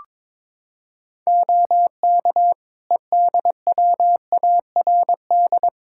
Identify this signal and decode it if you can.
Morse code, heard as a keyed tone, about 22 wpm, 700 Hz. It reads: OK EDWARD
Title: Practice QSO - Exchange 3 - More Info - 22wpm